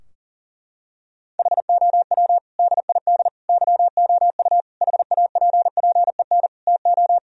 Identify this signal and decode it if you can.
Morse code, heard as a keyed tone, about 40 wpm, 700 Hz. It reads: HOW DID YOU HAPPEN TO
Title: Sentences from Top 1000 Words - Review Part 4 - 40wpm